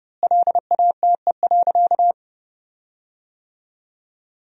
Morse code, heard as a keyed tone, about 31 wpm, 700 Hz. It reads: LATE.
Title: 4x Word-Spacing - Sentences from Top 300 Words - 30wpm